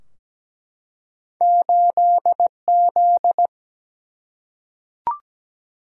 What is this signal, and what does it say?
Morse code, heard as a keyed tone, about 17 wpm, 700 Hz. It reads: 8Z
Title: Single-Letter Single-Number 17wpm